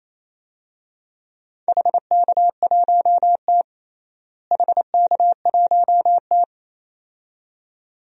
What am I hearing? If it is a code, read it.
Morse code, heard as a keyed tone, about 28 wpm, 700 Hz. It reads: HK1T HK1T
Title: DXCC-100 Call Signs - 4 Characters - Repeated 2x - 28wpm